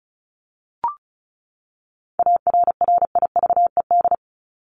Morse code, heard as a keyed tone, about 34 wpm, 700 Hz. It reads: ARRIVED
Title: ED Suffix - Top 100 Words with Base Words 35wpm